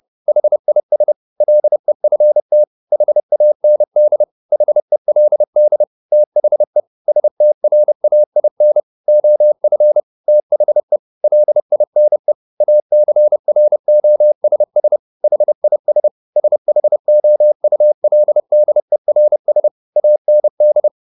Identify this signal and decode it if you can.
Morse code, heard as a keyed tone, about 30 wpm, 600 Hz. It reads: HIS LEFT HAND HELD THE STRAIN OF THE LINE ACROSS HIS SHOULDERS AND